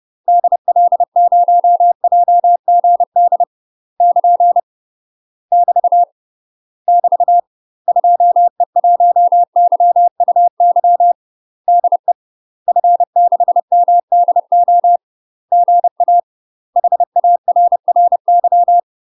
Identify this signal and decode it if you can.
Morse code, heard as a keyed tone, about 30 wpm, 700 Hz. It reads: DL0JGD <KN> <BT> <BT> 2E1YUY DE F6MDO GA HARRY
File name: UK MASTER QSOs TRAINING / 30WPM / UK MASTER QSOs 0010